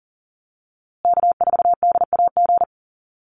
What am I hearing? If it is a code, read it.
Morse code, heard as a keyed tone, about 39 wpm, 700 Hz. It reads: K4DAG
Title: USA Call Signs Repeated 3x 40wpm